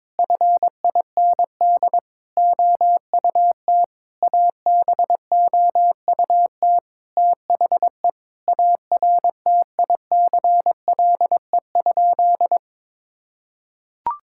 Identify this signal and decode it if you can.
Morse code, heard as a keyed tone, about 22 wpm, 700 Hz. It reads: FIND OUT ABOUT THE ARTICLE?
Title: Sentences from Top 1000 Words - Review Part 5 - 22wpm